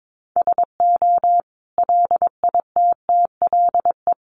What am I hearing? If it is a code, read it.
Morse code, heard as a keyed tone, about 22 wpm, 700 Hz. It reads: SO LITTLE
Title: Sentences from Top 1500 Words 22wpm